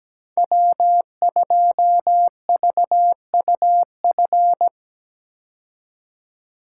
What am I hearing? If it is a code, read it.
Morse code, heard as a keyed tone, about 17 wpm, 700 Hz. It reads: W2VUF
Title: K1USN SST QSO Practice - 17wpm